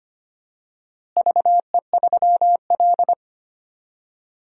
Morse code, heard as a keyed tone, about 25 wpm, 700 Hz. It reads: VE3L
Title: DXCC-100 Call Signs - 4 Characters - Repeated 2x - 25wpm